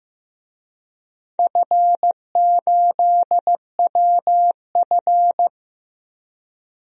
Morse code, heard as a keyed tone, about 15 wpm, 700 Hz. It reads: F8WF
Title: DXCC-100 Call Signs - 4 Characters - Repeated 2x - 15wpm